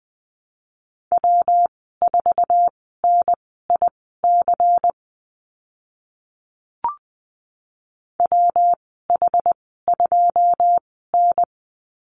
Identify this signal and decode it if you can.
Morse code, heard as a keyed tone, about 20 wpm, spaced as for 14 wpm, 700 Hz. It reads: W4NIC WH2N
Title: 122 Lesson 20g - Number 3 - Call Signs 20wpm